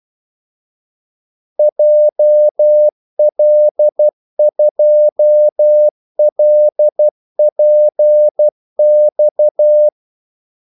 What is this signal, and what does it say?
Morse code, heard as a keyed tone, about 12 wpm, 600 Hz. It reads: JL2LPX
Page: Kaldesignaler 12 wpm